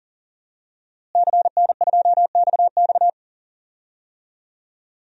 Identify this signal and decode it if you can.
Morse code, heard as a keyed tone, about 40 wpm, 700 Hz. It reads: CN2XX
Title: DXCC-100 Call Signs - 5 Characters - Repeated 3x - 40wpm